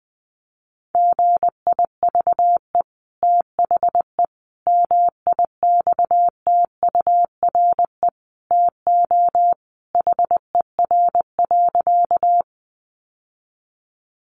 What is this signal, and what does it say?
Morse code, heard as a keyed tone, about 20 wpm, 700 Hz. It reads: GIVE THE MIXTURE TO HER.
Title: Sentences from Top 1500 Words 20wpm